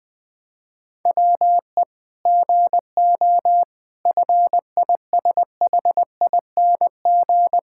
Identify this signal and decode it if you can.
Morse code, heard as a keyed tone, about 20 wpm, 700 Hz. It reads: WE GO FISHING
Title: Sentences from Top 1000 Words - Review Part 5 - 20wpm